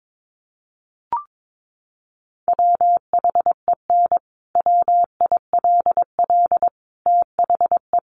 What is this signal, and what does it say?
Morse code, heard as a keyed tone, about 22 wpm, 700 Hz. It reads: WHEN WILL THE
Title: Sentences from Top 1300 Words 22wpm